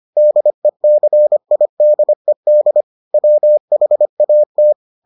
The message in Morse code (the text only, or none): DECIDED WHAT